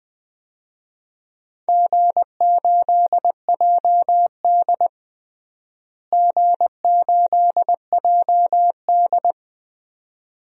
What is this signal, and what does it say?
Morse code, heard as a keyed tone, about 20 wpm, 700 Hz. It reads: G8JD G8JD
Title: DXCC-100 Call Signs - 4 Characters - Repeated 2x - 20wpm